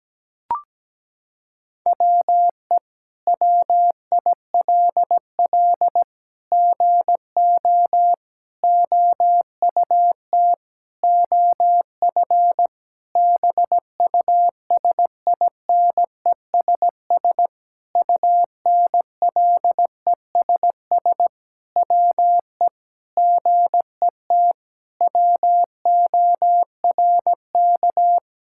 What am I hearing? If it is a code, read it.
Morse code, heard as a keyed tone, about 17 wpm, 700 Hz. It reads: WE WILL GO OUT OF BUSINESS UNLESS WE GET WORK
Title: Sentences from Top 1000 Words - Review Part 6 - 17wpm